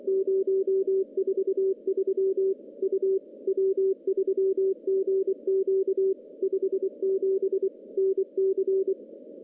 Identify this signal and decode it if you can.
Morse code, heard as a keyed tone, about 26 wpm, 380 Hz. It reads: 043 U W3GQ 57 NC
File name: contests / CW-SS-k5tr- -1500